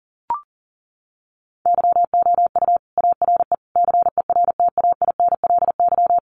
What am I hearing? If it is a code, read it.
Morse code, heard as a keyed tone, about 40 wpm, 700 Hz. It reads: YOU ARE CERTAINLY
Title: Sentences from Top 1200 Words 40wpm